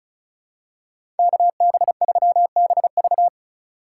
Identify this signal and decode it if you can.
Morse code, heard as a keyed tone, about 36 wpm, 700 Hz. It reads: KB3BV
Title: USA Call Signs - 5 Characters - 2x Character Spacing - 35wpm